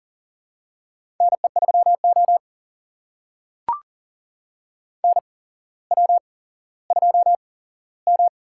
Morse code, heard as a keyed tone, about 42 wpm, spaced as for 17 wpm, 700 Hz. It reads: NE3O NW2M
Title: USA Call Signs - 4 Characters - 8x Character Spacing - 40wpm